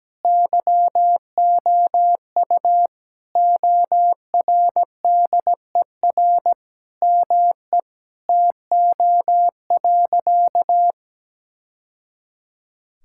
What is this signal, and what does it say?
Morse code, heard as a keyed tone, about 17 wpm, 700 Hz. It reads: YOU ORDER ME TO.
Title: Sentences from Top 1000 Words - Review Part 6 - 17wpm